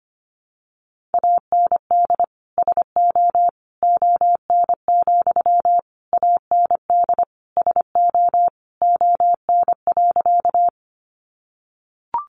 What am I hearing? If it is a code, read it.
Morse code, heard as a keyed tone, about 25 wpm, 700 Hz. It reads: AND SO ON, AND SO ON.
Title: Sentences from Top 100 Words 25wpm